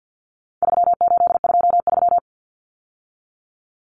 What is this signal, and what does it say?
Morse code, heard as a keyed tone, about 49 wpm, 700 Hz. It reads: 3823
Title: Common Cut Numbers - 4-Digit 50wpm